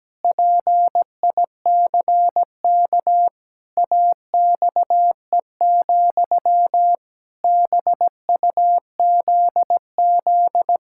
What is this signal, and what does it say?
Morse code, heard as a keyed tone, about 17 wpm, 700 Hz. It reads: PICK AXE, BUZZ
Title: Pangram Challenge v2 - 17wpm